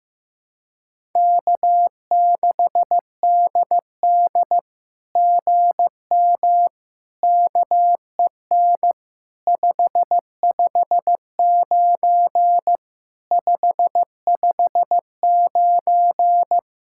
Morse code, heard as a keyed tone, about 15 wpm, 700 Hz. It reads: K6DD GM KEN 559 559
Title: SOTA QSO Practice 15wpm